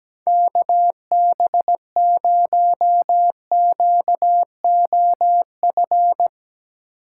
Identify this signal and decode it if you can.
Morse code, heard as a keyed tone, about 17 wpm, 700 Hz. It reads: KB0QOF
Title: USA Call Signs - 6 Characters - 3x Character Spacing - 17wpm